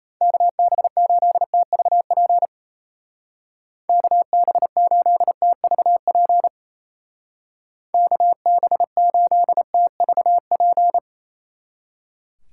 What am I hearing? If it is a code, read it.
Morse code, heard as a keyed tone, about 34 wpm, 700 Hz. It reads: KB8TVP KB8TVP KB8TVP
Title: Speed Racing - USA Call Signs 28wpm